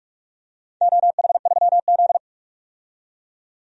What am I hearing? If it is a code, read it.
Morse code, heard as a keyed tone, about 44 wpm, 700 Hz. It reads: OH3Z
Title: DXCC-100 Call Signs - 4 Characters - 45wpm